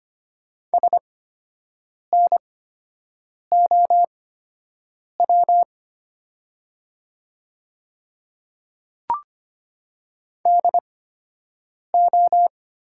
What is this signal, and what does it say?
Morse code, heard as a keyed tone, about 25 wpm, spaced as for 7 wpm, 700 Hz. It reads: SNOW DO
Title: 8x Character Spacing - Top 500 Words - 25wpm